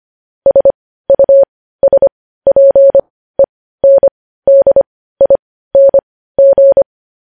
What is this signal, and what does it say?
Morse code, heard as a keyed tone, about 25 wpm, spaced as for 15 wpm, 550 Hz. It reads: SUSPENDING
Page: news-headlines-in-morse-code-at-15-wpm